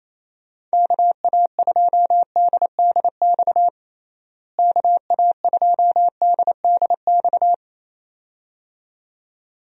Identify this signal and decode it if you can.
Morse code, heard as a keyed tone, about 27 wpm, 700 Hz. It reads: KA2DDX KA2DDX
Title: USA Call Signs - 6 Characters - Repeated 2x - 28wpm